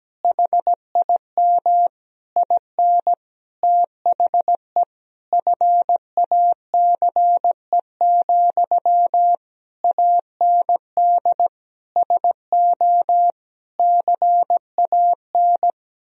Morse code, heard as a keyed tone, about 17 wpm, 700 Hz. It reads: HIM IN THE FACE, AND SO CAN
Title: Sentences from Top 1000 Words - Review Part 2 - 17wpm